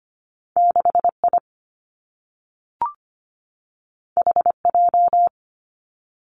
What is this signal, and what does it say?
Morse code, heard as a keyed tone, about 25 wpm, 700 Hz. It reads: 6I HJ
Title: Two Challenging Letter-Number Combinations 25wpm